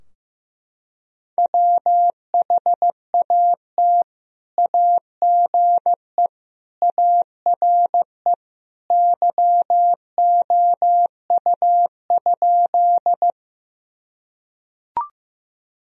Morse code, heard as a keyed tone, about 15 wpm, 700 Hz. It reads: WHAT AGE ARE YOU?
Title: Sentences from Top 1000 Words - Review Part 3 - 15wpm